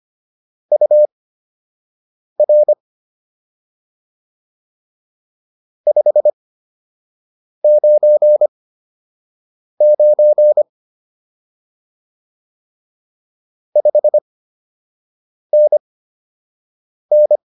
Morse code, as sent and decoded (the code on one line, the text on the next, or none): ..- .-. / ..... ----. ----. / ..... -. -.
UR 599 5NN